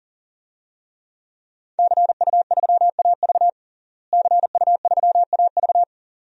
Morse code, heard as a keyed tone, about 37 wpm, 700 Hz. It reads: CU3AV CU3AV
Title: DXCC-100 Call Signs - 5 Characters - Repeated 2x - 40wpm